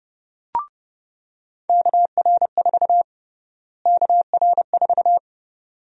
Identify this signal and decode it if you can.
Morse code, heard as a keyed tone, about 30 wpm, 700 Hz. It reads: KR4 KR4
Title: USA Call Sign Prefixes - Repeated 2x - 30wpm